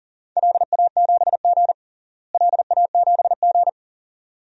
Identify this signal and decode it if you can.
Morse code, heard as a keyed tone, about 37 wpm, 700 Hz. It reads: LA7G LA7G
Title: DXCC-100 Call Signs - 4 Characters - Repeated 2x - 40wpm